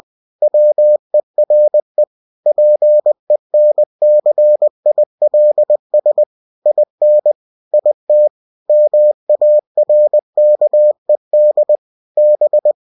Morse code, heard as a keyed tone, about 20 wpm, 600 Hz. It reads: WERE PENCILS IN IT MARKED B
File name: cw-books / winnie-the-pooh / winnie-20wpm-page-0109